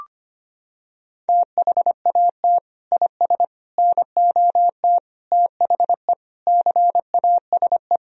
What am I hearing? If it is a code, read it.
Morse code, heard as a keyed tone, about 25 wpm, 700 Hz. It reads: THAT IS NOT THE CASE
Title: Sentences from Top 1000 Words - Review Part 2 - 25wpm